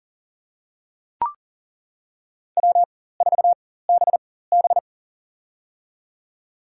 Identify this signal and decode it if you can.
Morse code, heard as a keyed tone, about 37 wpm, spaced as for 18 wpm, 700 Hz. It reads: W4BB
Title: USA Call Signs - 4 Characters - 4x Character Spacing - 40wpm